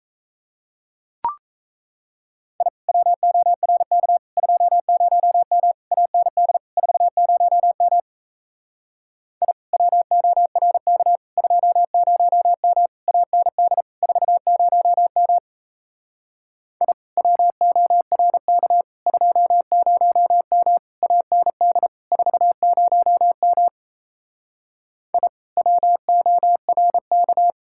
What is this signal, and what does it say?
Morse code, heard as a keyed tone, about 36 wpm, 700 Hz. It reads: I WORK 20M AND 40M I WORK 20M AND 40M I WORK 20M AND 40M I WORK